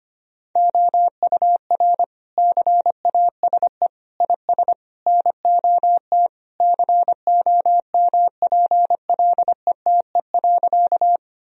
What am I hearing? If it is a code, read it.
Morse code, heard as a keyed tone, about 25 wpm, 700 Hz. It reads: OUR CASE IS NOT COMPLETE.
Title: Sentences from Top 1000 Words - Review Part 2 - 25wpm